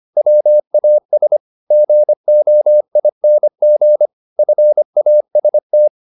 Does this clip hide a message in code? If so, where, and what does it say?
Yes: Morse code from 0.2 to 5.9 s, heard as a keyed tone, about 25 wpm, 600 Hz. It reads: WAS GOING FAST